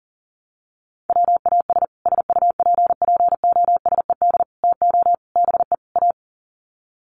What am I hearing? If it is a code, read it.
Morse code, heard as a keyed tone, about 40 wpm, 700 Hz. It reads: WAS SUPPOSED TO BE A
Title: Sets of 5 Words Encore 40wpm